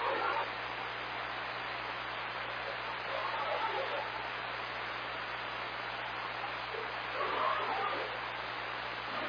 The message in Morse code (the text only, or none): none